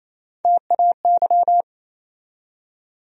Morse code, heard as a keyed tone, about 28 wpm, 700 Hz. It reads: TAY
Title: USA Call Sign Postfixes - 3 Letters 28wpm